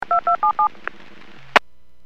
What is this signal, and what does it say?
DTMF, heard as telephone touch tones, tones of about 110 ms. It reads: 2277